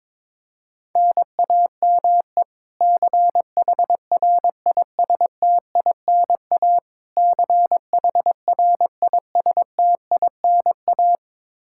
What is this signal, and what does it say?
Morse code, heard as a keyed tone, about 22 wpm, 700 Hz. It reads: NAME CHRISTINA CHRISTINA
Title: K1USN SST QSO Practice 22wpm